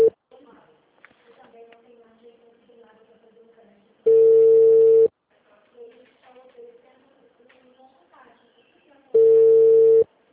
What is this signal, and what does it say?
Ringback tone (China, 450 Hz), with speech.
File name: out- -2024- - - .75654